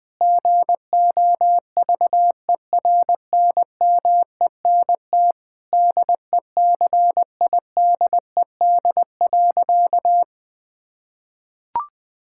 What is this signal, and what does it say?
Morse code, heard as a keyed tone, about 21 wpm, 700 Hz. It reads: GOVERNMENT DECIDED.
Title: Sentences with 3-Word Phrases - Spoken Beginning - 20wpm